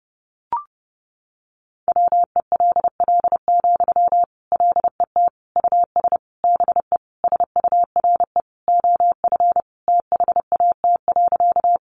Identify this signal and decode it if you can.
Morse code, heard as a keyed tone, about 30 wpm, 700 Hz. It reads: WELL, LET US BE SURE OF THAT.
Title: Sentences from Top 300 Words 30wpm